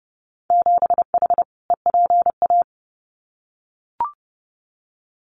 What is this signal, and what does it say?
Morse code, heard as a keyed tone, about 30 wpm, 700 Hz. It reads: 7H EPA
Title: Winter Field Day Exchanges Encore 30wpm